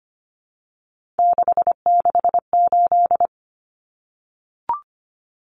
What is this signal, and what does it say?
Morse code, heard as a keyed tone, about 25 wpm, 700 Hz. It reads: 668
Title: Common Cut Numbers - 3-Digit 25wpm